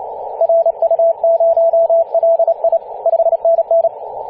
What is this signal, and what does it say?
Morse code, heard as a keyed tone, about 30 wpm, 670 Hz, with radio noise. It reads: RU0LI 5NN
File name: ep2a-21-029mhz-cw